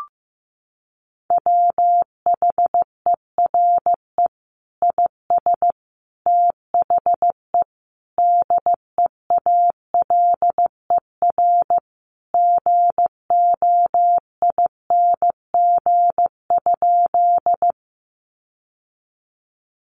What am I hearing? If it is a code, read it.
Morse code, heard as a keyed tone, about 15 wpm, 700 Hz. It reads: WHERE IS THE DEALER GOING?
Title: Sentences from Top 1100 Words 15wpm